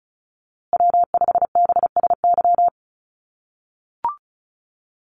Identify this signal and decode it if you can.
Morse code, heard as a keyed tone, about 34 wpm, 700 Hz. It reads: W5BSY
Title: USA Call Signs Repeated 2x 35wpm